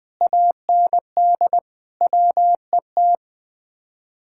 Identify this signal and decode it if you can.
Morse code, heard as a keyed tone, about 20 wpm, 700 Hz. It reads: AND WET
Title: Speed-Racing Common QSO Phrases - 17wpm